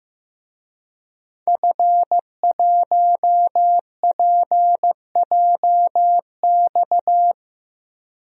Morse code, heard as a keyed tone, about 15 wpm, 700 Hz. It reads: F1PJX
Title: DXCC-100 Call Signs - 5 Characters - Repeated 3x - 15wpm